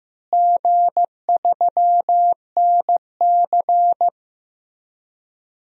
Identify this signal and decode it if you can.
Morse code, heard as a keyed tone, about 15 wpm, 700 Hz. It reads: G3NC
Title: DXCC-100 Call Signs - 4 Characters - Repeated 3x - 15wpm